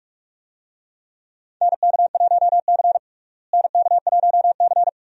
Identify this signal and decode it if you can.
Morse code, heard as a keyed tone, about 46 wpm, 700 Hz. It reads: NK1C NK1C
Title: USA Call Signs - 4 Characters - Repeated 2x - 45wpm